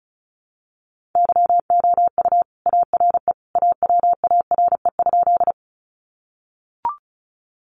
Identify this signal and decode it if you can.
Morse code, heard as a keyed tone, about 34 wpm, 700 Hz. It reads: YOU ARE AWARE?
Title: Sentences from Top 1200 Words 35wpm